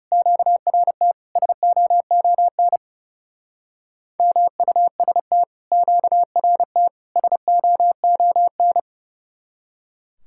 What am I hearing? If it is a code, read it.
Morse code, heard as a keyed tone, about 38 wpm, 700 Hz. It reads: QRT SOON MUST QRT SOON